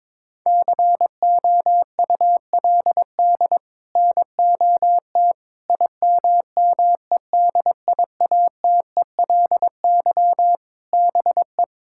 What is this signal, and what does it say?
Morse code, heard as a keyed tone, about 21 wpm, 700 Hz. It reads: COULD NOT IMMEDIATELY BE
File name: practice / 22wpm / Sets of 7 Words - 22wpm